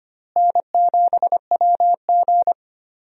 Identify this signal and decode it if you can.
Morse code, heard as a keyed tone, about 25 wpm, 700 Hz. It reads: N7WG
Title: USA Call Signs - 4 Characters - Repeated 2x - 25wpm